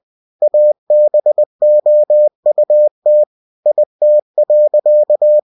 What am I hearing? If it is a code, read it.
Morse code, heard as a keyed tone, about 20 wpm, 600 Hz. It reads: ABOUT IT.
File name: cw-books / winnie-the-pooh / winnie-20wpm-page-0029